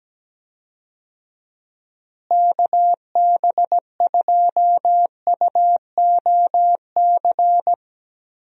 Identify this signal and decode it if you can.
Morse code, heard as a keyed tone, about 17 wpm, 700 Hz. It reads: KB2UOC
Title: USA Call Signs - 6 Characters - Repeated 2x - 17wpm